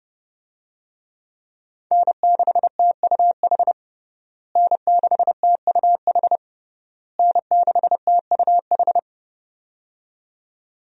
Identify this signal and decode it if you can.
Morse code, heard as a keyed tone, about 31 wpm, 700 Hz. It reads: N6TUH N6TUH N6TUH
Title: USA Call Signs - 5 Characters - Repeated 3x - 30wpm